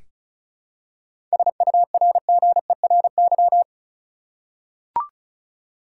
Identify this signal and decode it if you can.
Morse code, heard as a keyed tone, about 34 wpm, 700 Hz. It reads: SURGERY
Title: Seven Letter Words - 35wpm